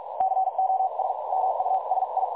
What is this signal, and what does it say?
Morse code, heard as a keyed tone, about 46 wpm, spaced as for 40 wpm, 760 Hz, with noise. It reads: COS MUSI
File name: dlazaawansowanych / nagraniazpasma / websdr recording 3565.0kHz